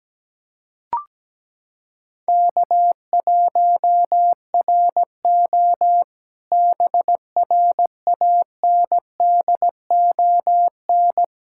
Morse code, heard as a keyed tone, about 17 wpm, 700 Hz. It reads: K1RO BRANDON
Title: K1USN SST QSO Practice - 17wpm